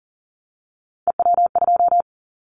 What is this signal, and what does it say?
Morse code, heard as a keyed tone, about 40 wpm, 700 Hz. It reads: EW2
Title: Callsign Prefixes for Top 100 DXCC - 40wpm